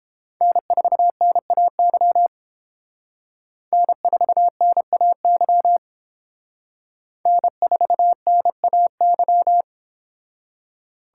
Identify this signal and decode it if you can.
Morse code, heard as a keyed tone, about 28 wpm, 700 Hz. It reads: N4NAY N4NAY N4NAY